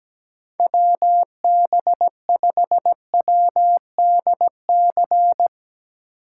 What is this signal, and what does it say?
Morse code, heard as a keyed tone, about 17 wpm, 700 Hz. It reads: WB5WDC
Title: USA Call Signs - 6 Characters - 4x Character Spacing - 17wpm